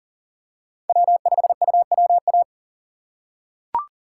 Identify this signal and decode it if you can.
Morse code, heard as a keyed tone, about 40 wpm, 700 Hz. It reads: W5UWA
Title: USA Call Signs - 5 Characters - 6x Character Spacing - 40wpm